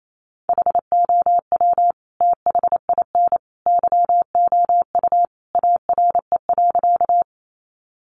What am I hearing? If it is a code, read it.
Morse code, heard as a keyed tone, about 28 wpm, 700 Hz. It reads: HOW THIN YOU ARE.
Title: Sentences from Top 1200 Words 28wpm